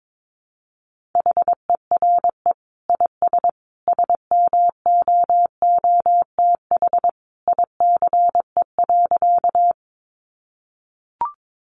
Morse code, heard as a keyed tone, about 22 wpm, 700 Hz. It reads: HERE IS SMOOTH ICE.
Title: Sentences from Top 1400 Words 22wpm